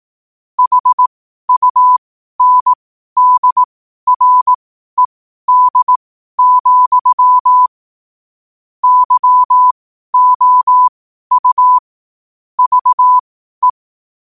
Morse code, heard as a keyed tone, about 18 wpm, spaced as for 12 wpm, 1000 Hz. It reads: HUNDRED, YOU VE
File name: qotdfiles / 12wpm